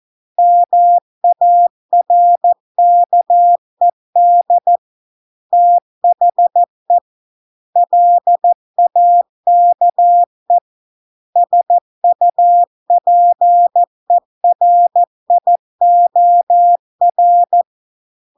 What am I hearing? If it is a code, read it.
Morse code, heard as a keyed tone, about 14 wpm, 700 Hz. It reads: MARKED THE LAKE SUPERIOR